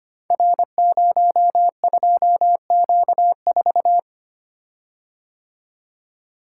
Morse code, heard as a keyed tone, about 25 wpm, 700 Hz. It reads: R02Q4
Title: Code Groups - 5 Letter:Numbers - 25wpm